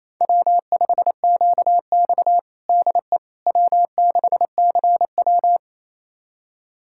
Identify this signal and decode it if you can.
Morse code, heard as a keyed tone, about 27 wpm, 700 Hz. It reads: W5QX DE W6CW
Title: USA Callsign DE Callsign - 4 Characters - 28wpm